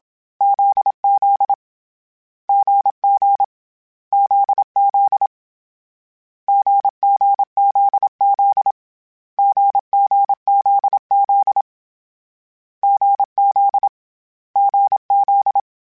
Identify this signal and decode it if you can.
Morse code, heard as a keyed tone, about 27 wpm, 800 Hz. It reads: ZZ GG ZZ GGZZ GGZZ GZ GZ